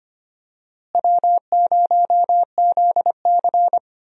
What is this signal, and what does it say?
Morse code, heard as a keyed tone, about 25 wpm, 700 Hz. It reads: W0ZC
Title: USA Call Signs - 4 Characters - Repeated 3x - 25wpm